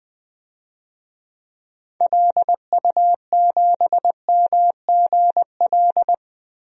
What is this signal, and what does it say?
Morse code, heard as a keyed tone, about 20 wpm, 700 Hz. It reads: LU7MGL
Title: DXCC-100 Call Signs - 6 Characters - Repeated 2x - 20wpm